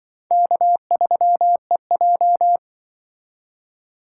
Morse code, heard as a keyed tone, about 25 wpm, 700 Hz. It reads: K3EJ
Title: Speed Racing - USA 1x2 Call Signs 20wpm